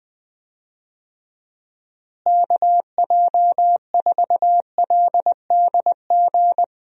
Morse code, heard as a keyed tone, about 20 wpm, 700 Hz. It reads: KJ4LDG
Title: USA Call Signs - 6 Characters - Repeated 3x - 20wpm